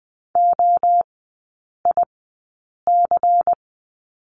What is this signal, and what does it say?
Morse code, heard as a keyed tone, about 20 wpm, spaced as for 8 wpm, 700 Hz. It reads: OIC
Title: 133 Lesson 22e - Question Mark - Three Characters 20wpm